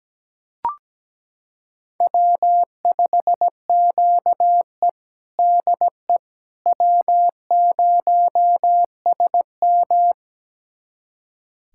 Morse code, heard as a keyed tone, about 17 wpm, 700 Hz. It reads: W5QE DE W0SM
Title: USA Callsign DE Callsign - 4 Characters - 17wpm